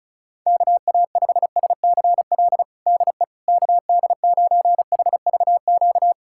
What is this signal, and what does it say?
Morse code, heard as a keyed tone, about 36 wpm, 700 Hz. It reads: KA5SCL DE KD9HVQ
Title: USA Callsign DE Callsign - 6 Characters - 35wpm